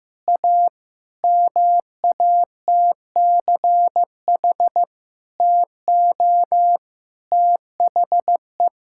Morse code, heard as a keyed tone, about 15 wpm, 700 Hz. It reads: A MATCH TO THE
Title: Sentences from Top 1600 Words 15wpm